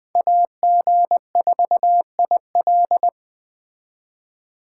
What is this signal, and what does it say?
Morse code, heard as a keyed tone, about 19 wpm, 700 Hz. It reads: AG4IL